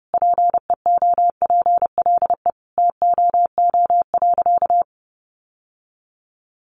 Morse code, heard as a keyed tone, about 28 wpm, 700 Hz. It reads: PEOPLE TOO.
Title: Sentences from Top 1000 Words 30wpm